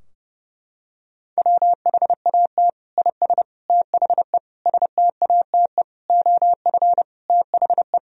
Morse code, heard as a keyed tone, about 32 wpm, 700 Hz. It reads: WHAT IS THE STATE OF THE
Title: Sentences from Top 1000 Words - Review Part 3 - 30wpm